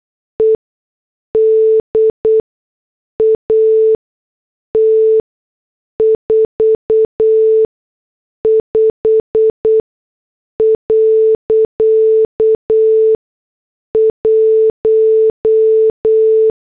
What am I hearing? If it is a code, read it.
Morse code, heard as a keyed tone, about 8 wpm, spaced as for 6 wpm, 440 Hz. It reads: ED AT 45.1